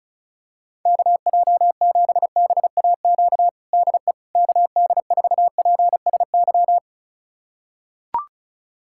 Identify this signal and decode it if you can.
Morse code, heard as a keyed tone, about 36 wpm, 700 Hz. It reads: KJ7BAQ DE KD4PSY
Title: USA Callsign DE Callsign 35wpm